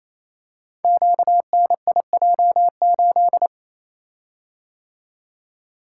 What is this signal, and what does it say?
Morse code, heard as a keyed tone, about 28 wpm, 700 Hz. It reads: QNIJ8
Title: Code Groups - 5 Letter:Numbers - 28wpm